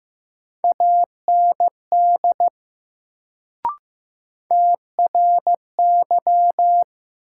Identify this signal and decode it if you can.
Morse code, heard as a keyed tone, about 15 wpm, 700 Hz. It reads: AND TRY
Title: Sets of 2 Words - New Word Drill - 15wpm